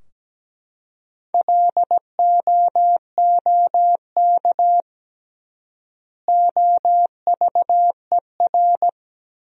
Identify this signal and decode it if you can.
Morse code, heard as a keyed tone, about 17 wpm, 700 Hz. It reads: LOOK OVER
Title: 3x Word-Spacing - Sentences from Top 100 Words - 17wpm